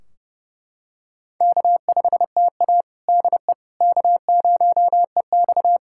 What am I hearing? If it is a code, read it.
Morse code, heard as a keyed tone, about 31 wpm, 700 Hz. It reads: K5TA DE K0EX
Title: Practice QSO - Exchange 4 - Ending - 30wpm